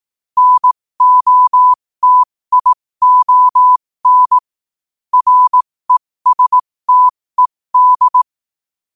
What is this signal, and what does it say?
Morse code, heard as a keyed tone, about 18 wpm, spaced as for 15 wpm, 1000 Hz. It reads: NOTION RESTED
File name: qotdfiles / 15wpm